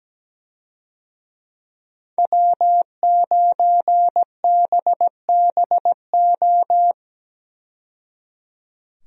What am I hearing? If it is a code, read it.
Morse code, heard as a keyed tone, about 17 wpm, 700 Hz. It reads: W9BBO
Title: USA Call Signs - 5 Characters - 17wpm